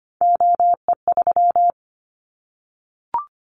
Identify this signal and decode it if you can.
Morse code, heard as a keyed tone, about 25 wpm, 700 Hz. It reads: OE3
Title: Callsign Prefixes for Top 100 DXCC - 25wpm